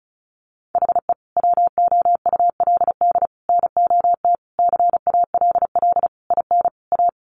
Morse code, heard as a keyed tone, about 34 wpm, 700 Hz. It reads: HE WOULD NOT CALL IN A
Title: Sentences from Top 1500 Words 35wpm